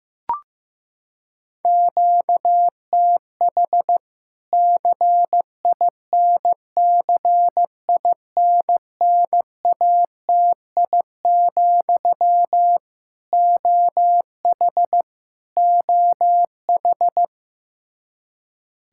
Morse code, heard as a keyed tone, about 15 wpm, 700 Hz. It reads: QTH CINCINNATI, OH OH
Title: K1USN SST QSO Practice 15wpm